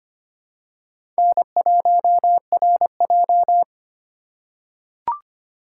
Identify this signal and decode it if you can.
Morse code, heard as a keyed tone, about 25 wpm, 700 Hz. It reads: N1RJ
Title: USA Call Signs - 4 Characters - 2x Character Spacing - 25wpm